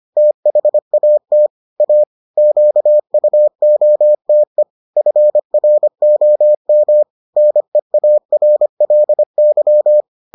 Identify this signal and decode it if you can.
Morse code, heard as a keyed tone, about 25 wpm, 600 Hz. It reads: THAT A QUOTE FROM NEARLY